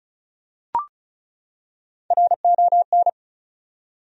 Morse code, heard as a keyed tone, about 36 wpm, 700 Hz. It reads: RON
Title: USA Common Names - Max 4 Letters - 35wpm